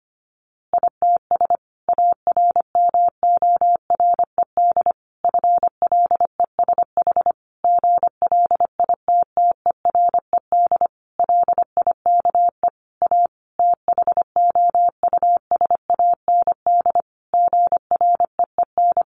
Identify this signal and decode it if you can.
Morse code, heard as a keyed tone, about 25 wpm, 700 Hz. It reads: ITS ARMORED FLESH GLITTERED LIKE A THOUSAND GREEN
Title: A Sound Of Thunder 25wpm